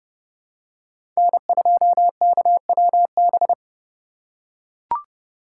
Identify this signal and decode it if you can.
Morse code, heard as a keyed tone, about 31 wpm, 700 Hz. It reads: N2KWB
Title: USA Call Signs - 5 Characters - 2x Character Spacing - 30wpm